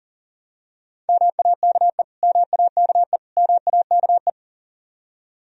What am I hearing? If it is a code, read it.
Morse code, heard as a keyed tone, about 40 wpm, 700 Hz. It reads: MAKE MAKE MAKE
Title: Top 100 Words Repeated 40wpm